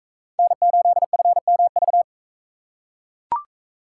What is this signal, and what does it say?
Morse code, heard as a keyed tone, about 44 wpm, 700 Hz. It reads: N8FMV